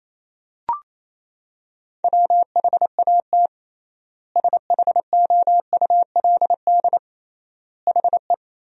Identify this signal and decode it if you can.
Morse code, heard as a keyed tone, about 28 wpm, 700 Hz. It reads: WHAT SHOULD HE
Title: 3x Word-Spacing - Sentences from Top 300 Words - 28wpm